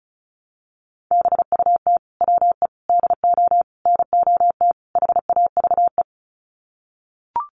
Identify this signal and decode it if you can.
Morse code, heard as a keyed tone, about 34 wpm, 700 Hz. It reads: BUT WE DO NOT HAVE
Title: Sets of 5 Words Encore2 35wpm